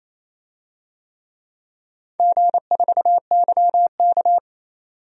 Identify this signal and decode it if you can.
Morse code, heard as a keyed tone, about 28 wpm, 700 Hz. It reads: G4YK
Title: DXCC-100 Call Signs - 4 Characters - 28wpm